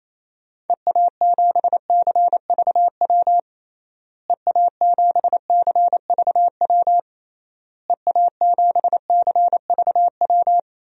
Morse code, heard as a keyed tone, about 27 wpm, 700 Hz. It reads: EA7CVW EA7CVW EA7CVW
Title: DXCC-100 Call Signs - 6 Characters - Repeated 3x - 28wpm